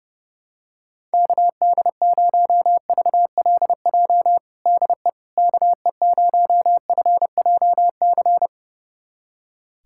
Morse code, heard as a keyed tone, about 32 wpm, 700 Hz. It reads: KD0VLJ DE KE0FJC
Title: USA Callsign DE Callsign - 6 Characters - 30wpm